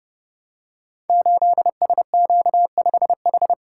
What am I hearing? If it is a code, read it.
Morse code, heard as a keyed tone, about 30 wpm, 700 Hz. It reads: 8SQ5H
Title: Code Groups - 5 Letter:Numbers - 30wpm